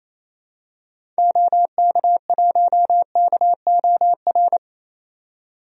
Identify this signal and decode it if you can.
Morse code, heard as a keyed tone, about 28 wpm, 700 Hz. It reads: OK1KOR
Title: DXCC-100 Call Signs - 6 Characters - 28wpm